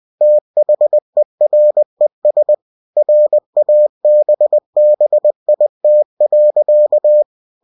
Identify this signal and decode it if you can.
Morse code, heard as a keyed tone, about 20 wpm, 600 Hz. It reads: THERES RABBIT.
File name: cw-books / winnie-the-pooh / winnie-20wpm-page-0091